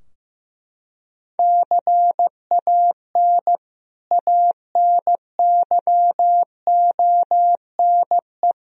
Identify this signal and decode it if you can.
Morse code, heard as a keyed tone, about 15 wpm, 700 Hz. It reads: CAN ANYONE
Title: Sentences from Top 1000 Words - Review Part 4 - 15wpm